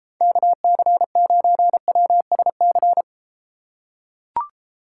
Morse code, heard as a keyed tone, about 32 wpm, 700 Hz. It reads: KC9WSC